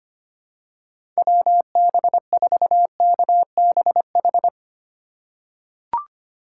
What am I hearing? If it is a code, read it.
Morse code, heard as a keyed tone, about 25 wpm, 700 Hz. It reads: WB4KBH
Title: USA Call Signs - 6 Characters - 3x Character Spacing - 25wpm